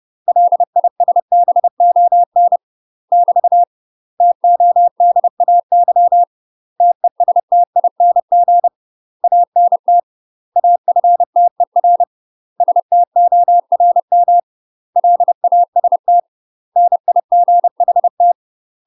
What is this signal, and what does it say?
Morse code, heard as a keyed tone, about 30 wpm, 700 Hz. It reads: LISBON <BT> TODAY TESTING ANT AFTER STORM LAST NIGHT
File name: UK MASTER QSOs TRAINING / 30WPM / UK MASTER QSOs 0033